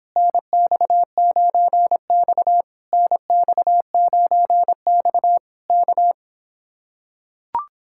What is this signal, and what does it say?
Morse code, heard as a keyed tone, about 26 wpm, 700 Hz. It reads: NX9X NX9X K